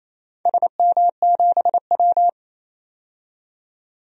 Morse code, heard as a keyed tone, about 27 wpm, 700 Hz. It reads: SM7W
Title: DXCC-100 Call Signs - 4 Characters - 28wpm